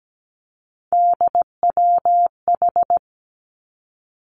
Morse code, heard as a keyed tone, about 17 wpm, 700 Hz. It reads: DWH
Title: Three Challenging Letter-Number Combinations 17wpm